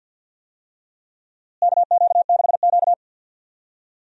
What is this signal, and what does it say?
Morse code, heard as a keyed tone, about 48 wpm, 700 Hz. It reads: KQ6Q
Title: USA Call Signs - 4 Characters - Repeated 3x - 50wpm